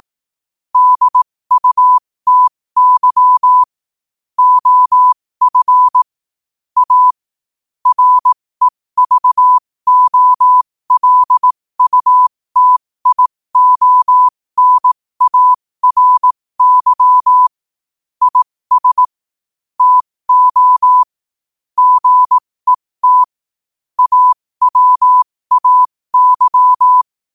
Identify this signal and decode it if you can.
Morse code, heard as a keyed tone, about 18 wpm, spaced as for 15 wpm, 1000 Hz. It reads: DUTY OF A REVOLUTIONARY IS TO GET AWAY
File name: qotdfiles / 15wpm